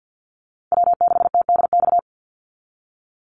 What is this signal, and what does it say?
Morse code, heard as a keyed tone, about 49 wpm, 700 Hz. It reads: W6TDX
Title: NA SOTA Chaser Callsigns 50wpm